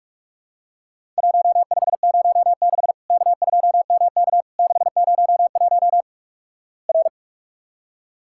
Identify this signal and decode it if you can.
Morse code, heard as a keyed tone, about 46 wpm, 700 Hz. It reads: 1506 K2MK 601 R
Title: Worked All Europe DX Contest - QTC Exchange - 45wpm